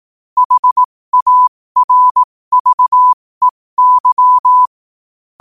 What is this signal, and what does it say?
Morse code, heard as a keyed tone, about 18 wpm, spaced as for 15 wpm, 1000 Hz. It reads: HARVEY
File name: qotdfiles / 15wpm